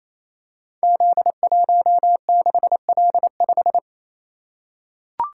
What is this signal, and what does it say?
Morse code, heard as a keyed tone, about 28 wpm, 700 Hz. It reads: Z16L5
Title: Code Groups - 5 Letter:Numbers - 28wpm